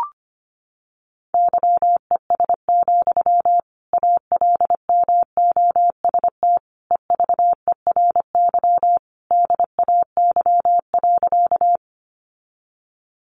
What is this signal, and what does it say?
Morse code, heard as a keyed tone, about 25 wpm, 700 Hz. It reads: YES, ALMOST EVERY DAY.
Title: Sentences from Top 700 Words 25wpm